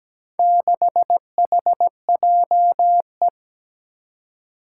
Morse code, heard as a keyed tone, about 17 wpm, 700 Hz. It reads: 6HJE
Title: Four Challenging Letter-Number Combinations 17wpm